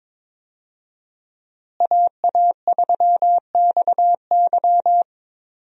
Morse code, heard as a keyed tone, about 22 wpm, 700 Hz. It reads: AA3XY
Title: USA Call Signs - 5 Characters - Repeated 2x - 22wpm